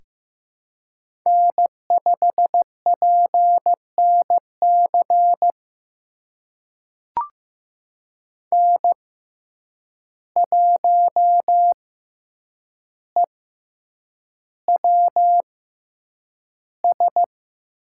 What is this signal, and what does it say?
Morse code, heard as a keyed tone, about 15 wpm, spaced as for 8 wpm, 700 Hz. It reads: N5PNC N1EWS
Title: USA Call Signs - 5 Characters - 6x Character Spacing - 15wpm